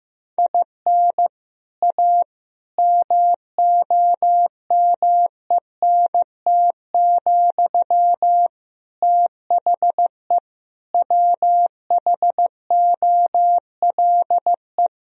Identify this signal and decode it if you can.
Morse code, heard as a keyed tone, about 15 wpm, 700 Hz. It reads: IN A MOMENT, THE WHOLE
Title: Sentences from Top 1000 Words - Review Part 3 - 15wpm